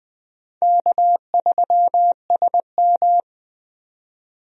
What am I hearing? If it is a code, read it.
Morse code, heard as a keyed tone, about 20 wpm, 700 Hz. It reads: K3SM
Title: USA Call Signs - 4 Characters - 20wpm